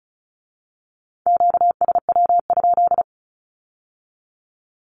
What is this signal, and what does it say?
Morse code, heard as a keyed tone, about 34 wpm, 700 Hz. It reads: QSW?
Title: Q-Codes with Spoken Meaning - 35wpm